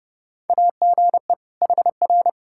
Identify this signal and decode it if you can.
Morse code, heard as a keyed tone, about 30 wpm, 700 Hz. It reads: AGE HR
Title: K1USN SST QSO Practice 30wpm